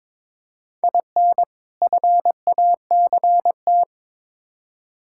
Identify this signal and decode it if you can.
Morse code, heard as a keyed tone, about 22 wpm, 700 Hz. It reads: IN FACT
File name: Build A Phrase - 2-Word Phrases - 22wpm